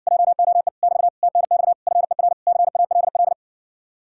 Morse code, heard as a keyed tone, about 60 wpm, 700 Hz. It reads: JOE <BT> TNX FER CALL